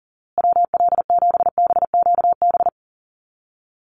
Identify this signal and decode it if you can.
Morse code, heard as a keyed tone, about 40 wpm, 700 Hz. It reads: WL7BQB
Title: USA Call Signs Repeated 3x 40wpm